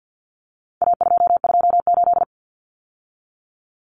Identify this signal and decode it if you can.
Morse code, heard as a keyed tone, about 50 wpm, 700 Hz. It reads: A228
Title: Common Cut Numbers - 4-Digit 50wpm